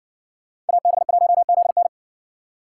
Morse code, heard as a keyed tone, about 55 wpm, 700 Hz. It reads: AB1ZN